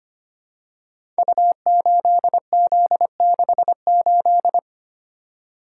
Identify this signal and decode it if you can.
Morse code, heard as a keyed tone, about 25 wpm, 700 Hz. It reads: U8Z68
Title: Code Groups - 5 Letter:Numbers - 25wpm